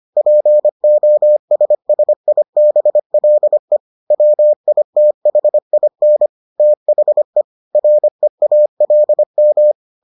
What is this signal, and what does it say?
Morse code, heard as a keyed tone, about 25 wpm, 600 Hz. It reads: POSSIBLE WITHIN THE REALM